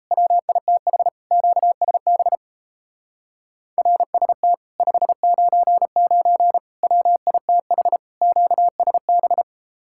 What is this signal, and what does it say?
Morse code, heard as a keyed tone, about 37 wpm, 700 Hz. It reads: WITH QSB RST 599 WITH QSB